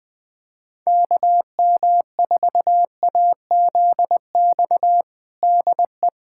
Morse code, heard as a keyed tone, about 20 wpm, 700 Hz. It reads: KM4AZX DE
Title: USA Callsign DE Callsign - 6 Characters - 20wpm